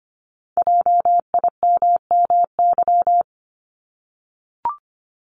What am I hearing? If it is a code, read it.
Morse code, heard as a keyed tone, about 25 wpm, 700 Hz. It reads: JIMMY
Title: Multi-letter Phonemes 25wpm